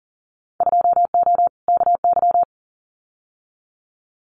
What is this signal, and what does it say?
Morse code, heard as a keyed tone, about 41 wpm, 700 Hz. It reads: 2O KY
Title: Winter Field Day Exchanges 40wpm